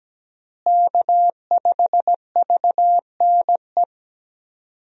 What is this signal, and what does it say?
Morse code, heard as a keyed tone, about 17 wpm, 700 Hz. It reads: K5VNE
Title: USA Call Signs - 5 Characters - Repeated 2x - 17wpm